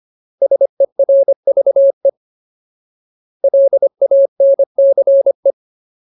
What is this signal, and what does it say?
Morse code, heard as a keyed tone, about 25 wpm, 550 Hz. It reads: SERVE LANCE
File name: Ultimate Word List / 25WPM Ultimate